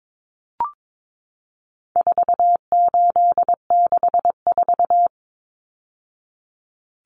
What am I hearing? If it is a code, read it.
Morse code, heard as a keyed tone, about 22 wpm, 700 Hz. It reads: 4864
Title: Numbers - 4-Digit 22wpm